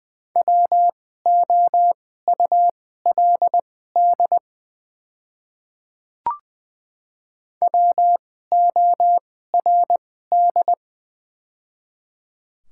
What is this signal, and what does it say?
Morse code, heard as a keyed tone, about 20 wpm, spaced as for 14 wpm, 690 Hz. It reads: WOULD WORD
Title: 2x Character Spacing - Top 100 Words - 20wpm